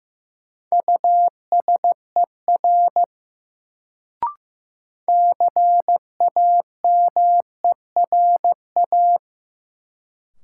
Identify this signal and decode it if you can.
Morse code, heard as a keyed tone, about 15 wpm, 700 Hz. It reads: USER CAMERA
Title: Sentences from Top 1000 Words - Review Part 5 - New Word Drill - 15wpm